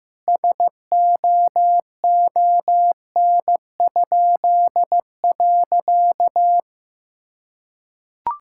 Morse code, heard as a keyed tone, about 16 wpm, 700 Hz. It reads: SOON?.
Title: Pangram Challenge v2 - 15wpm